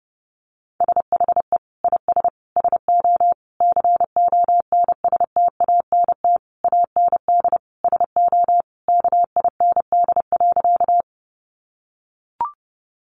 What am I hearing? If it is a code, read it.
Morse code, heard as a keyed tone, about 30 wpm, 700 Hz. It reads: SHE IS SO CONSTANT AND SO KIND.
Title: Sentences from Top 1500 Words 30wpm